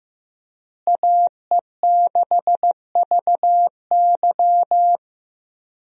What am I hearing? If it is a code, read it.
Morse code, heard as a keyed tone, about 15 wpm, 700 Hz. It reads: AE6VY
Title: USA Call Signs - 5 Characters - Repeated 3x - 15wpm